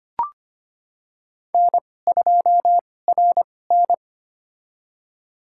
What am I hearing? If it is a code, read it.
Morse code, heard as a keyed tone, about 25 wpm, spaced as for 18 wpm, 700 Hz. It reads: N2RN
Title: USA Call Signs - 2x Character Spacing - 25wpm